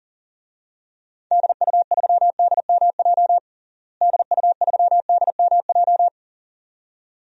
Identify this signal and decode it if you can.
Morse code, heard as a keyed tone, about 37 wpm, 700 Hz. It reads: DU3DMJ DU3DMJ
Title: DXCC-100 Call Signs - 6 Characters - Repeated 2x - 40wpm